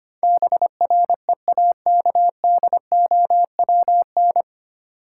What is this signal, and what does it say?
Morse code, heard as a keyed tone, about 25 wpm, 700 Hz. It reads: BREAKDOWN
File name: Nine Letter Words - 25wpm